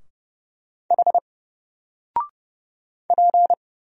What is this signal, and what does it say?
Morse code, heard as a keyed tone, about 30 wpm, 700 Hz. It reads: H P
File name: Single-Letter-Number-30wpm